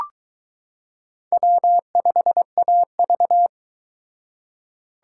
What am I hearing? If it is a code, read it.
Morse code, heard as a keyed tone, about 22 wpm, 700 Hz. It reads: W5AV